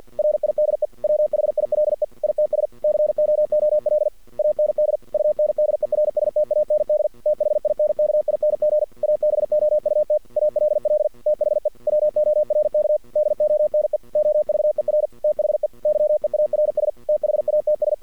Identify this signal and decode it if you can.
Morse code, heard as a keyed tone, about 53 wpm, 610 Hz. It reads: RIDE WHILE ITS COOL AND UNDERSTAND THINKING ABOUT ALL THE WORK DONE OVER THE YEARS THATS
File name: archived / CRSnet